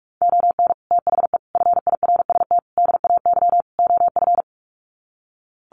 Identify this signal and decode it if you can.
Morse code, heard as a keyed tone, about 44 wpm, 700 Hz. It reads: ON THE FIRST DAY OF
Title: Sets of 5 Words Encore 45wpm